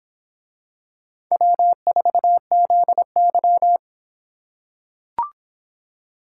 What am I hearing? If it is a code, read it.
Morse code, heard as a keyed tone, about 26 wpm, 700 Hz. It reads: W4ZY